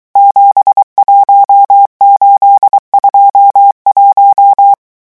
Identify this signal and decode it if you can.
Morse code, heard as a keyed tone, about 23 wpm, 800 Hz. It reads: 71821
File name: morse mp3 / revision11 chiffres vitesse 22